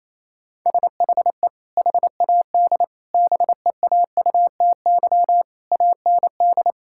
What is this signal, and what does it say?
Morse code, heard as a keyed tone, about 28 wpm, 700 Hz. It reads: SHE HAD BEAUTY AND
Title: Sentences from Top 1000 Words - Review Part 5 - 28wpm